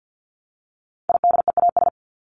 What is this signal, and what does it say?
Morse code, heard as a keyed tone, about 48 wpm, 700 Hz. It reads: IDEAS
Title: ES:S Suffix - Top 100 Words with Base Words 50wpm